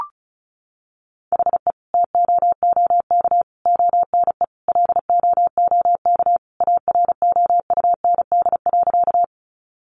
Morse code, heard as a keyed tone, about 34 wpm, 700 Hz. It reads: HE TOOK ONE LOOK AROUND.
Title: Sentences from Top 600 Words 35wpm